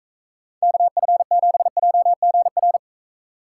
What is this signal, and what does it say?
Morse code, heard as a keyed tone, about 43 wpm, 700 Hz. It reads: KF7JGR